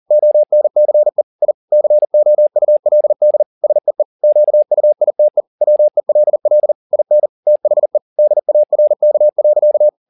Morse code, heard as a keyed tone, about 40 wpm, 600 Hz. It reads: ONCE I COULD SEE QUITE WELL IN THE DARK.